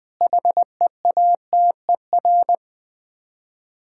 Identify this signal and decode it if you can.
Morse code, heard as a keyed tone, about 20 wpm, 700 Hz. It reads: HEATER
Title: Difficult Words with Hints - 20wpm